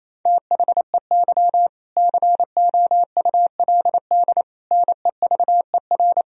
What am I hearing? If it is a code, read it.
Morse code, heard as a keyed tone, about 28 wpm, 700 Hz. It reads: THEY COULD NEVER
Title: Sentences from Top 1000 Words - Review Part 4 - 28wpm